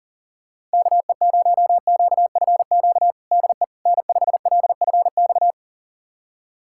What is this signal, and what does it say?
Morse code, heard as a keyed tone, about 40 wpm, 700 Hz. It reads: KE0QFQ DE N5LFX
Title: USA Callsign DE Callsign 40wpm